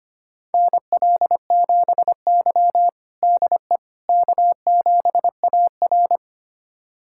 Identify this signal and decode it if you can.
Morse code, heard as a keyed tone, about 25 wpm, 700 Hz. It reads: NL7Y DE K7AR
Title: USA Callsign DE Callsign - 4 Characters - 25wpm